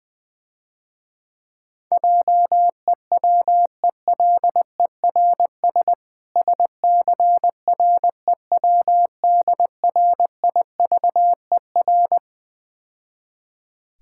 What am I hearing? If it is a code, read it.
Morse code, heard as a keyed tone, about 20 wpm, 700 Hz. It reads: JEWELERS SCREWDRIVER
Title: Difficult Words - Speed-Racing with Hints - 20wpm